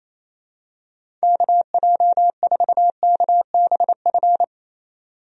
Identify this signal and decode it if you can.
Morse code, heard as a keyed tone, about 28 wpm, 700 Hz. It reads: KJ4KBF
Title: USA Call Signs - 6 Characters - 28wpm